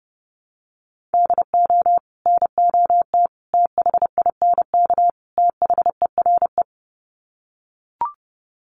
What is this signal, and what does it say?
Morse code, heard as a keyed tone, about 30 wpm, 700 Hz. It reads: DO NOT THINK THERE
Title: Sets of 4 Words 30wpm